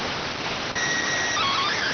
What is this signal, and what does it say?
SSTV (slow-scan television), PD 290.